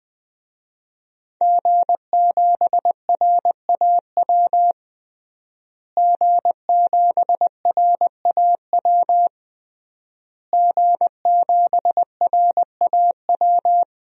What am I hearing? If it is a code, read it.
Morse code, heard as a keyed tone, about 20 wpm, 700 Hz. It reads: G7RAW G7RAW G7RAW
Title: DXCC-100 Call Signs - 5 Characters - Repeated 3x - 20wpm